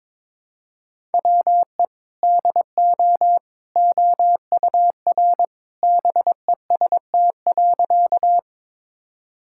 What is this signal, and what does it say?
Morse code, heard as a keyed tone, about 22 wpm, 700 Hz. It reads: WE DO OUR BEST.
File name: practice / 22wpm / Sentences with 2-Word Phrases - 22wpm